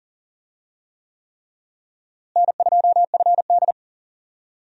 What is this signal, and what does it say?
Morse code, heard as a keyed tone, about 37 wpm, 700 Hz. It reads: N2FD
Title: USA Call Signs - 4 Characters - Repeated 3x - 40wpm